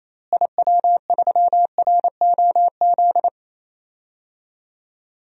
Morse code, heard as a keyed tone, about 28 wpm, 700 Hz. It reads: IW3ROZ
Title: DXCC-100 Call Signs - 6 Characters - Repeated 2x - 28wpm